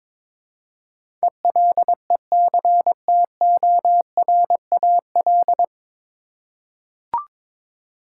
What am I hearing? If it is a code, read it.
Morse code, heard as a keyed tone, about 22 wpm, 700 Hz. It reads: ELECTORAL
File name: 9 Letter Words - First 3 Letters Given - 22wpm